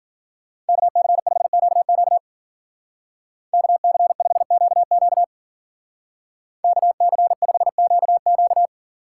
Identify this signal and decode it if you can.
Morse code, heard as a keyed tone, about 48 wpm, 700 Hz. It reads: KC5QQ KC5QQ KC5QQ
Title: Speed Racing - USA 2x2 Call Signs 40wpm